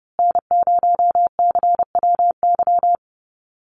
Morse code, heard as a keyed tone, about 30 wpm, 700 Hz. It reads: N0CWY
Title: USA Call Signs Repeated 3x 30wpm